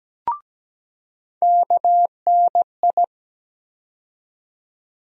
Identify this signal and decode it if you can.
Morse code, heard as a keyed tone, about 17 wpm, 700 Hz. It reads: KNI
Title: USA Call Sign Postfixes - 3 Letters 17wpm